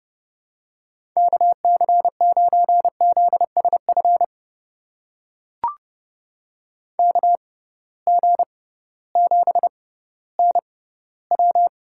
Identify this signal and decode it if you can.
Morse code, heard as a keyed tone, about 30 wpm, spaced as for 16 wpm, 700 Hz. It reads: KC9ZSF KG7NW
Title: USA Call Signs - 6x Character Spacing - 30wpm